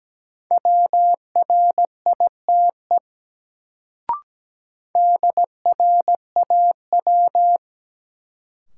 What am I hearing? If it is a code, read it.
Morse code, heard as a keyed tone, about 17 wpm, 700 Hz. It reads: WRITE DRAW
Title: Sentences from Top 300 Words - New Word Drill - 17wpm